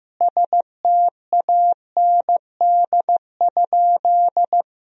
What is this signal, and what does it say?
Morse code, heard as a keyed tone, about 15 wpm, 700 Hz. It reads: STAND?
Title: 2x Word-Spacing - Sentences from Top 300 Words - 15wpm